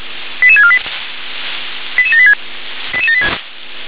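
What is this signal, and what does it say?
Five-tone selective call (ZVEI), heard as quick runs of five tones, tones of about 70 ms. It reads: 88648 88786
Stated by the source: ZVEI five-tone selective call received by SDR.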